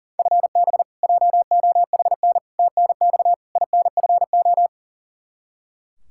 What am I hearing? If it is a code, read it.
Morse code, heard as a keyed tone, about 40 wpm, 700 Hz. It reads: FB JOHN TNX INFO
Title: Practice QSO - Exchange 3 - More Info - 40wpm